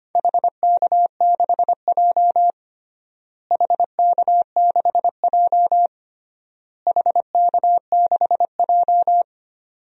Morse code, heard as a keyed tone, about 25 wpm, 700 Hz. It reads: HK6J HK6J HK6J
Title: DXCC-100 Call Signs - 4 Characters - Repeated 3x - 25wpm